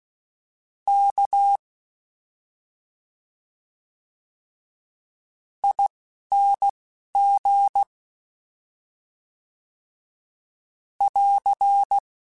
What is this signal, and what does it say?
Morse code, heard as a keyed tone, about 16 wpm, spaced as for 5 wpm, 800 Hz. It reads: KING <AR>
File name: Special INGs-mp3s / Special INGs-mp3s / Slow ing @10